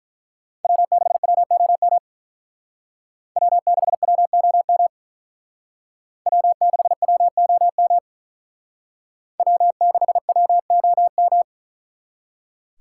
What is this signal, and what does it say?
Morse code, heard as a keyed tone, about 42 wpm, 700 Hz. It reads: W6WOM W6WOM W6WOM W6WOM
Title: Speed Racing - USA Call Signs 35wpm